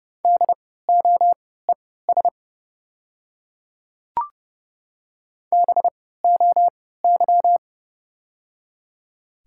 Morse code, heard as a keyed tone, about 30 wpm, spaced as for 17 wpm, 700 Hz. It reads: DOES BOY
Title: 3x Character Spacing - Top 200 Words - 30wpm